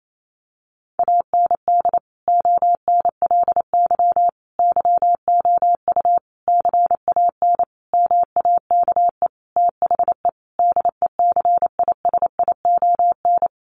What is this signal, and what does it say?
Morse code, heard as a keyed tone, about 28 wpm, 700 Hz. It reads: AND ONLY YOU CAN MAKE THE DECISION
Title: Sentences from Top 1400 Words 28wpm